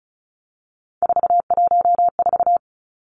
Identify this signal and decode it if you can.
Morse code, heard as a keyed tone, about 34 wpm, 700 Hz. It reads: 414
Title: Numbers - 3-Digit 35wpm